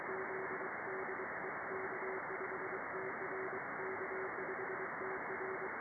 Morse code, heard as a keyed tone, about 23 wpm, 380 Hz, with radio noise. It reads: Z DE K5TR K5TR
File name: contests / sept-vhf-k5tr- -2200